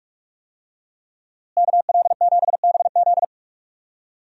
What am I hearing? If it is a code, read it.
Morse code, heard as a keyed tone, about 44 wpm, 700 Hz. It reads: KL7BZ
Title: DXCC-100 Call Signs - 5 Characters - Repeated 2x - 45wpm